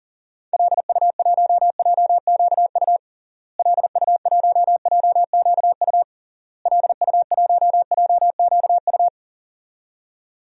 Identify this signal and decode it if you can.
Morse code, heard as a keyed tone, about 40 wpm, 700 Hz. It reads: LU1JQU LU1JQU LU1JQU
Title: DXCC-100 Call Signs - 6 Characters - Repeated 3x - 40wpm